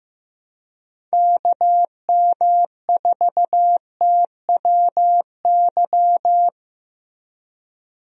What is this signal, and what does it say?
Morse code, heard as a keyed tone, about 15 wpm, 700 Hz. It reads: KM4TWY
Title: USA Call Signs - 6 Characters - Repeated 2x - 15wpm